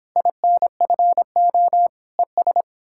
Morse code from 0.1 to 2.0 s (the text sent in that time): INFO